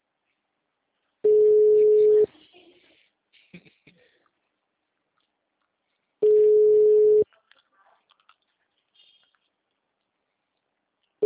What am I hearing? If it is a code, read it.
Ringback tone (Europe, 425 Hz), with speech.
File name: out- -2001- - - .93884